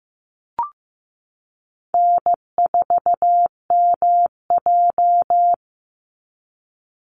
Morse code, heard as a keyed tone, about 15 wpm, 700 Hz. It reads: N4MJ
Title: NA SOTA Chaser Callsigns 15wpm